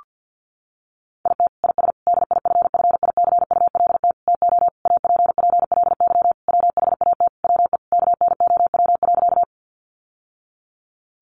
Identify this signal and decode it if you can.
Morse code, heard as a keyed tone, about 50 wpm, 700 Hz. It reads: IT IS DIFFICULT TO APPLY WHAT WE KNOW.
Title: Sentences from Top 1200 Words 50wpm